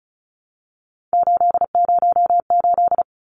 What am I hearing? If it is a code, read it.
Morse code, heard as a keyed tone, about 34 wpm, 700 Hz. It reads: 808
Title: Numbers - 3-Digit 35wpm